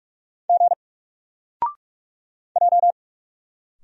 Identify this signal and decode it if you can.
Morse code, heard as a keyed tone, about 44 wpm, 700 Hz. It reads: G J
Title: Single Character (Letter:Number:Punctuation:Prosign) 45wpm